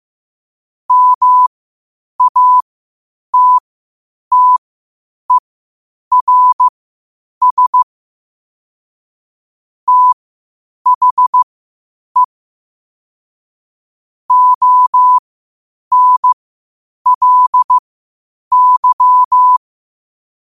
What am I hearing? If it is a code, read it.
Morse code, heard as a keyed tone, about 15 wpm, spaced as for 8 wpm, 1000 Hz. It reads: MATTERS THE ONLY
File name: qotdfiles / 8wpm